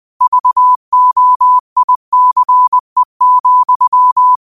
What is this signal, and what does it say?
Morse code, heard as a keyed tone, about 20 wpm, 1000 Hz. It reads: VOICE,
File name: qotdfiles / 20wpm